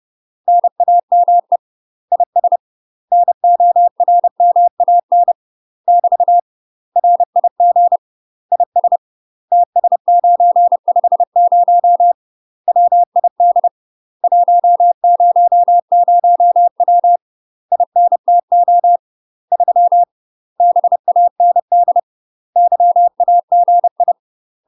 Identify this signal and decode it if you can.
Morse code, heard as a keyed tone, about 30 wpm, 700 Hz. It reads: NAME IS NORMAN <BT> RIG IS TS950 WID 100W INTO 3 BAND YAGI
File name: UK MASTER QSOs TRAINING/30WPM